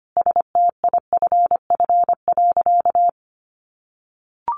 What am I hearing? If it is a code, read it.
Morse code, heard as a keyed tone, about 25 wpm, 700 Hz. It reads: STIFF.
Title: A Sound Of Thunder 25wpm